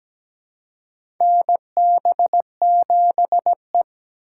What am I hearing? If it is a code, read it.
Morse code, heard as a keyed tone, about 17 wpm, 700 Hz. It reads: NB7E
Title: USA Call Signs - 4 Characters - 4x Character Spacing - 17wpm